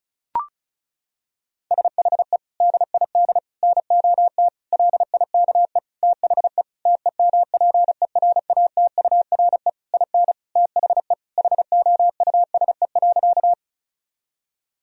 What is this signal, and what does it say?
Morse code, heard as a keyed tone, about 34 wpm, 700 Hz. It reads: SHE DID NOT LIKE THE TEMPERATURE IN THE HOUSE.
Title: Sentences from Top 1000 Words - Review Part 4 - 35wpm